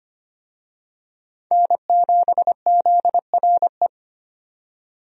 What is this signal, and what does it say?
Morse code, heard as a keyed tone, about 25 wpm, 700 Hz. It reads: N7ZRE
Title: USA Call Signs - 5 Characters - 25wpm